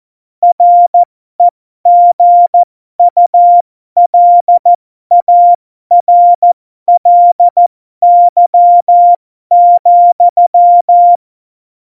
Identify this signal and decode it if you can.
Morse code, heard as a keyed tone, about 14 wpm, spaced as for 12 wpm, 700 Hz. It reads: REGULARLY,
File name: Morsekurs / Witze